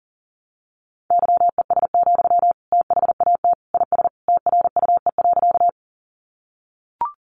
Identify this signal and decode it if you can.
Morse code, heard as a keyed tone, about 40 wpm, 700 Hz. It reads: YES, THAT IS TRUE.
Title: Sentences from Top 500 Words 40wpm